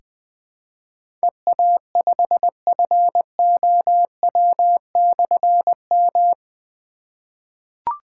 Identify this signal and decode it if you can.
Morse code, heard as a keyed tone, about 20 wpm, 700 Hz. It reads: EA5FOW/M
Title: Difficult Call Signs 20wpm